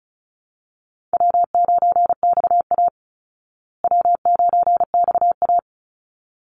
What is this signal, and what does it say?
Morse code, heard as a keyed tone, about 34 wpm, 700 Hz. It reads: W9XA W9XA
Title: Easy Call Signs Repeated 2x - 35wpm